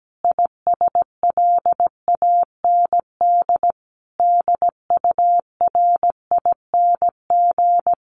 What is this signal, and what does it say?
Morse code, heard as a keyed tone, about 17 wpm, 700 Hz. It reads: ISLAND DURING
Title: Sentences from Top 1400 Words 17wpm